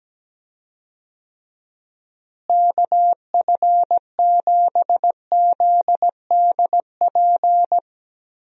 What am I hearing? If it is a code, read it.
Morse code, heard as a keyed tone, about 17 wpm, 700 Hz. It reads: KF7ZDP
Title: USA Call Signs - 6 Characters - Repeated 3x - 17wpm